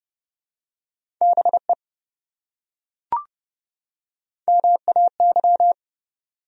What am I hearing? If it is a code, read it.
Morse code, heard as a keyed tone, about 26 wpm, 700 Hz. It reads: BE MAY